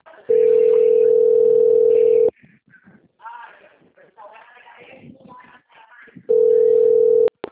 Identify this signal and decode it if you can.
Ringback tone (North America), 440 + 480 Hz, with speech.
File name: out- - - - - .3780